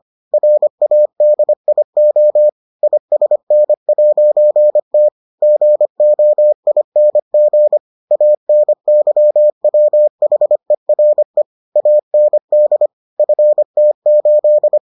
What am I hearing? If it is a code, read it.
Morse code, heard as a keyed tone, about 25 wpm, 600 Hz. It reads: RADIO ISN'T GOING ANYWHERE AND FT8